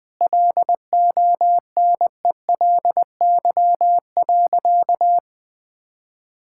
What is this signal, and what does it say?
Morse code, heard as a keyed tone, about 20 wpm, 700 Hz. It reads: LONELY.
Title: Sentences with 6-Word Phrases - 20wpm